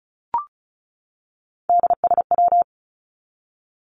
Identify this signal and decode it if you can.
Morse code, heard as a keyed tone, about 34 wpm, 700 Hz. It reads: DSW
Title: Abbreviations 35wpm